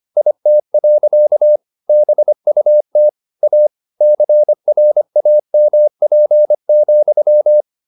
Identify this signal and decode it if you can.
Morse code, heard as a keyed tone, about 25 wpm, 600 Hz. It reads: IT. BUT A CRAMP,
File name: the old man and the sea part 1 25wpm